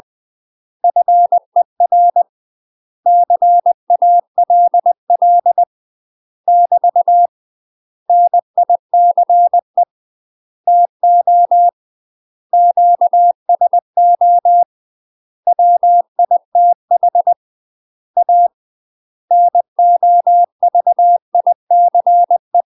Morse code, heard as a keyed tone, about 20 wpm, 700 Hz. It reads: FER CALL <BT> NICE TO QSO WITH A NOVICE
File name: UK MASTER QSOs TRAINING / 20WPM / UK MASTER QSOs 0046